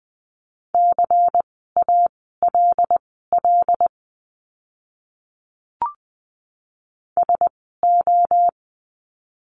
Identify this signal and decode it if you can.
Morse code, heard as a keyed tone, about 20 wpm, spaced as for 15 wpm, 700 Hz. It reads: CALL SO
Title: 230 Lesson 37f - Letter x - Words 20wpm